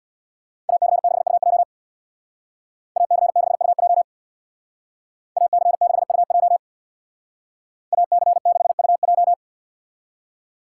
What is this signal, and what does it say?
Morse code, heard as a keyed tone, about 64 wpm, 700 Hz. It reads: AC6UJ AC6UJ AC6UJ AC6UJ